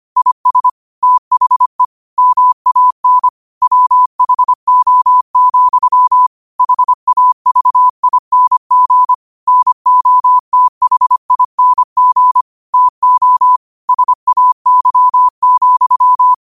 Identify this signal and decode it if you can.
Morse code, heard as a keyed tone, about 25 wpm, 1000 Hz. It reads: IS THE MAN WHO, HAVING NOTHING TO SAY,
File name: qotdfiles / 25wpm